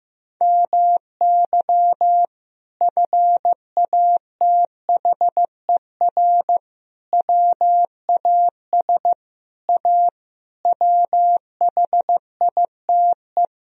Morse code, heard as a keyed tone, about 16 wpm, 700 Hz. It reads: MY FATHER WAS A WHITE
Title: Sentences from Top 1000 Words - Review Part 1 - 15wpm